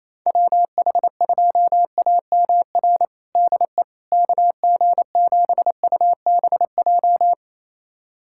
Morse code, heard as a keyed tone, about 28 wpm, 700 Hz. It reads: WH2AMR DE KG7UBJ
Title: USA Callsign DE Callsign - 6 Characters - 28wpm